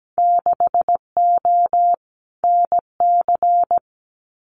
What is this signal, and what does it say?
Morse code, heard as a keyed tone, about 17 wpm, 700 Hz. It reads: 6O NC
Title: Winter Field Day Exchanges 17wpm